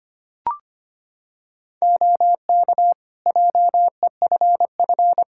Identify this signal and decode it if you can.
Morse code, heard as a keyed tone, about 25 wpm, 700 Hz. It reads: OK JEFF
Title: K1USN SST QSO Practice 25wpm